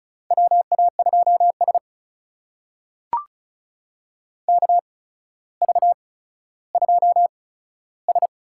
Morse code, heard as a keyed tone, about 36 wpm, spaced as for 15 wpm, 700 Hz. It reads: WA2S KV2S
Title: USA Call Signs - 4 Characters - 8x Character Spacing - 35wpm